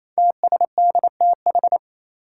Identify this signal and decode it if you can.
Morse code, heard as a keyed tone, about 27 wpm, 700 Hz. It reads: TSDTH
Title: Five Challenging Letter-Number Combinations 28wpm